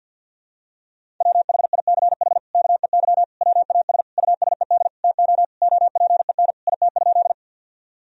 Morse code, heard as a keyed tone, about 50 wpm, 700 Hz. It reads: WHICH KEY WAS USED TO OPEN IT?
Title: Sentences from Top 1000 Words - Review Part 4 - 50wpm